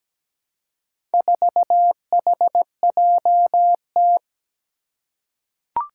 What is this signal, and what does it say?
Morse code, heard as a keyed tone, about 17 wpm, 700 Hz. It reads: 4HJT
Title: Four Challenging Letter-Number Combinations 17wpm